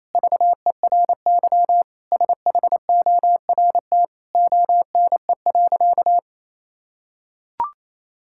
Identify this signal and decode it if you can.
Morse code, heard as a keyed tone, about 28 wpm, 700 Hz. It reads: VERY SHORT ONE.
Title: Sentences from Top 1600 Words 28wpm